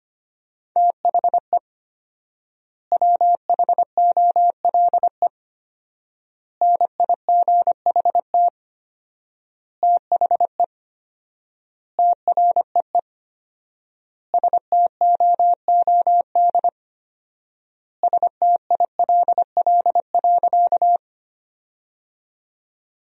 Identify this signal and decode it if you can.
Morse code, heard as a keyed tone, about 25 wpm, 700 Hz. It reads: THE WHOLE NIGHT THE TREE STOOD STILL.
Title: 4x Word-Spacing - Sentences from Top 500 Words - 25wpm